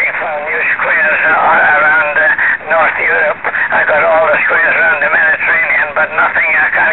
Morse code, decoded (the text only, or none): none